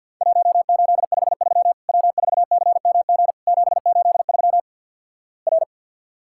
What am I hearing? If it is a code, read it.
Morse code, heard as a keyed tone, about 50 wpm, 700 Hz. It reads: 1853 W4CMG 783 R
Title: Worked All Europe DX Contest - QTC Exchange - 50wpm